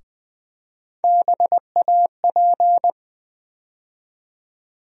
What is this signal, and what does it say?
Morse code, heard as a keyed tone, about 20 wpm, 700 Hz. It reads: BAP
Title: USA Call Sign Postfixes - 3 Letters 20wpm